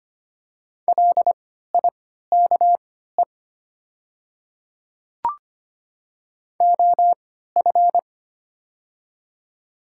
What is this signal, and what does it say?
Morse code, heard as a keyed tone, about 25 wpm, spaced as for 14 wpm, 700 Hz. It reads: LIKE OF
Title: 3x Character Spacing - Top 200 Words - 25wpm